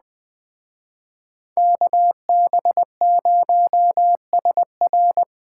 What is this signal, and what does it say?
Morse code, heard as a keyed tone, about 20 wpm, 700 Hz. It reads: KB0SR
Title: Speed Racing - USA 2x2 Call Signs 17wpm